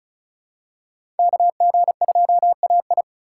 Morse code, heard as a keyed tone, about 36 wpm, 700 Hz. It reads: KG2AI
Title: USA Call Signs - 5 Characters - 3x Character Spacing - 35wpm